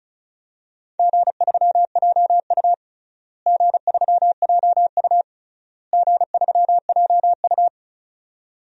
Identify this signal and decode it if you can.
Morse code, heard as a keyed tone, about 34 wpm, 700 Hz. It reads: G3JU G3JU G3JU
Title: DXCC-100 Call Signs - 4 Characters - Repeated 3x - 35wpm